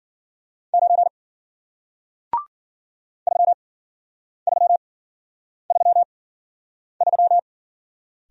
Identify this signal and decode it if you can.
Morse code, heard as a keyed tone, about 53 wpm, spaced as for 9 wpm, 700 Hz. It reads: 9 3333